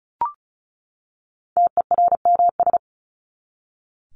Morse code, heard as a keyed tone, about 34 wpm, 700 Hz. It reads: TERMS
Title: ES:S Suffix - Top 100 Words with Base Words 35wpm